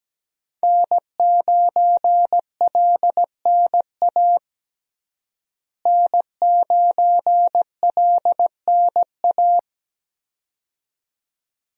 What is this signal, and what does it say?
Morse code, heard as a keyed tone, about 17 wpm, 700 Hz. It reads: N9LNA N9LNA
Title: USA Call Signs - 5 Characters - Repeated 2x - 17wpm